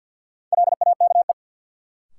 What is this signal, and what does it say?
Morse code, heard as a keyed tone, about 53 wpm, 700 Hz. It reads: LAKE
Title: Sentences from Top 1000 Words - Review Part 3 - New Word Drill - 50wpm